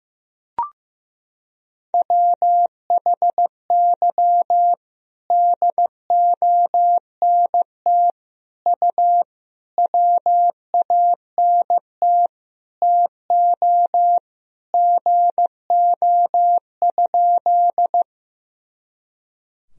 Morse code, heard as a keyed tone, about 16 wpm, 700 Hz. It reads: WHY DONT U WANT TO GO?
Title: Sentences from Top 300 Words and Abbreviations 15wpm